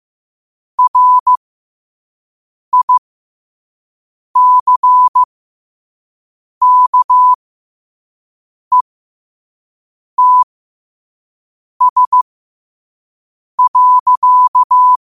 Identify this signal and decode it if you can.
Morse code, heard as a keyed tone, about 15 wpm, spaced as for 5 wpm, 1000 Hz. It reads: RICKETS.
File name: qotdfiles / 5wpm